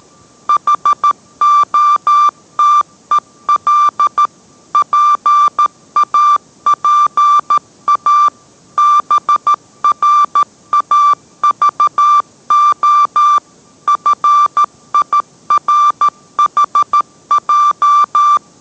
Morse code, heard as a keyed tone, about 13 wpm, 1180 Hz. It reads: HOTEL PAPA BRAVO FIRHJ